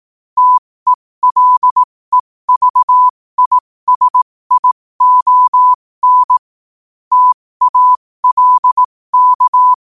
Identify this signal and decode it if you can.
Morse code, heard as a keyed tone, about 18 wpm, spaced as for 15 wpm, 1000 Hz. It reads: TELEVISION TALK
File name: qotdfiles / 15wpm